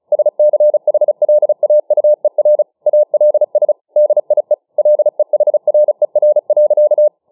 Morse code, heard as a keyed tone, about 35 wpm, 600 Hz. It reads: SCHLAUER ALS DIE LEHRER.